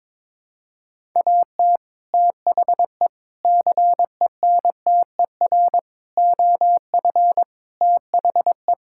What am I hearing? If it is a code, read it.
Morse code, heard as a keyed tone, about 22 wpm, 700 Hz. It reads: AT THE CENTER OF THE
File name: practice / 22wpm / Sentences with 4-Word Phrases - 22wpm